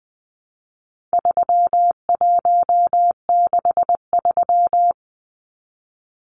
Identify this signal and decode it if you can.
Morse code, heard as a keyed tone, about 20 wpm, 700 Hz. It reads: 3163
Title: Numbers - 4-Digit 20wpm